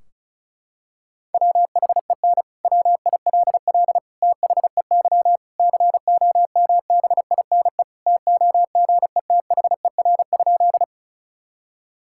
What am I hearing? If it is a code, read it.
Morse code, heard as a keyed tone, about 36 wpm, 700 Hz. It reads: WHEN WILL THEY COMBINE TOGETHER?
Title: Sentences from Top 1600 Words 35wpm